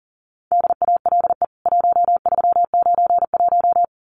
Morse code, heard as a keyed tone, about 40 wpm, 700 Hz. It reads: DALE 1391
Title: CWT Exchange 40wpm